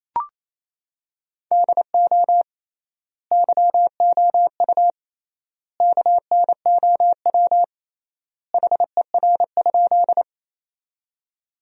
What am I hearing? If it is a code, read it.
Morse code, heard as a keyed tone, about 28 wpm, 700 Hz. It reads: DO YOU KNOW HER?
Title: 3x Word-Spacing - Sentences from Top 100 Words - 28wpm